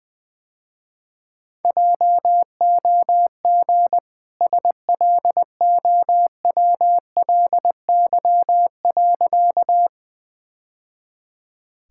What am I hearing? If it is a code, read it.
Morse code, heard as a keyed tone, about 21 wpm, 700 Hz. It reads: JOG SLOWLY.
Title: Easy Sentences with Hints - 20wpm